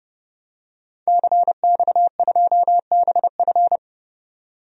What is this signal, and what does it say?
Morse code, heard as a keyed tone, about 31 wpm, 700 Hz. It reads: CX2BF
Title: DXCC-100 Call Signs - 5 Characters - 30wpm